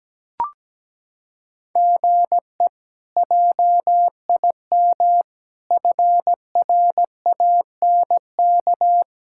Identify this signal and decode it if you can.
Morse code, heard as a keyed tone, about 17 wpm, 700 Hz. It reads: GE JIM FRANK
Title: K1USN SST QSO Practice - 17wpm